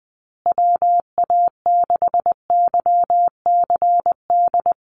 Morse code, heard as a keyed tone, about 20 wpm, 700 Hz. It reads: WA6YCD
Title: USA Call Signs Repeated 2x 20wpm